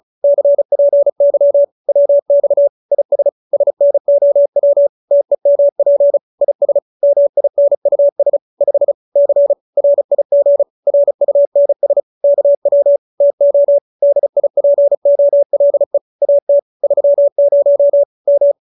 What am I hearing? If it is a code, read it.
Morse code, heard as a keyed tone, about 34 wpm, 570 Hz. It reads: CPY WX IS SNOW TEMP IS MINUS 5 C RIG RUNS KW TO DIPOLE AT 30 M